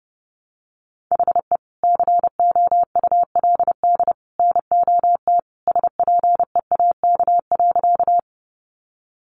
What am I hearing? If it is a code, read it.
Morse code, heard as a keyed tone, about 30 wpm, 700 Hz. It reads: HE COULD NOT SPEAK.
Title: Sentences from Top 1100 Words 30wpm